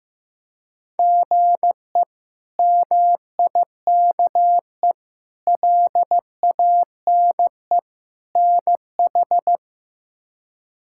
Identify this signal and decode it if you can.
Morse code, heard as a keyed tone, about 15 wpm, 700 Hz. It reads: GE MIKE LANE NH
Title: K1USN SST QSO Practice - 15wpm